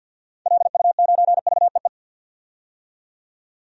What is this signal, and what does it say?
Morse code, heard as a keyed tone, about 46 wpm, 700 Hz. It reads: LU9VEE
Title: DXCC-100 Call Signs - 6 Characters - 50wpm